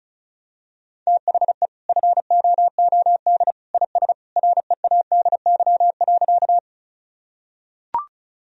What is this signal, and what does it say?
Morse code, heard as a keyed tone, about 36 wpm, 700 Hz. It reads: THE FOOD IS READY.
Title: Sentences from Top 1000 Words - Review Part 1 - 35wpm